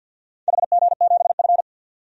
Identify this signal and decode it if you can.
Morse code, heard as a keyed tone, about 50 wpm, 700 Hz. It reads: HG7F
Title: DXCC-100 Call Signs - 4 Characters - 50wpm